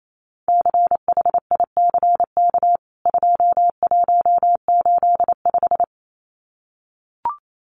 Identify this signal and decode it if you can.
Morse code, heard as a keyed tone, about 28 wpm, 700 Hz. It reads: CHICK 2185
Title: CWT Exchange 28wpm